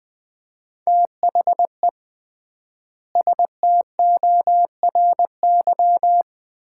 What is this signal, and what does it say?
Morse code, heard as a keyed tone, about 21 wpm, 700 Hz. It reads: THE STORY
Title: 3x Word-Spacing - Sets of 3 Words - 20wpm